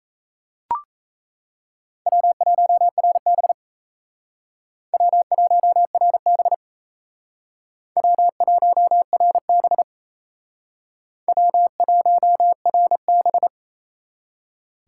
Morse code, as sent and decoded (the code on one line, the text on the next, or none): .-- .---- .-. -... / .-- .---- .-. -... / .-- .---- .-. -... / .-- .---- .-. -...
W1RB W1RB W1RB W1RB